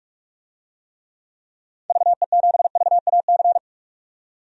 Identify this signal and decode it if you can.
Morse code, heard as a keyed tone, about 42 wpm, 700 Hz. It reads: VE7VAC
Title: DXCC-100 Call Signs - 6 Characters - 45wpm